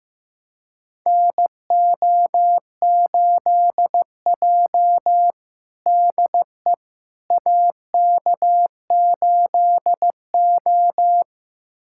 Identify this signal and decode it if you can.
Morse code, heard as a keyed tone, about 15 wpm, 700 Hz. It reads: NO8J DE AK8O
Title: USA Callsign DE Callsign - 4 Characters - 15wpm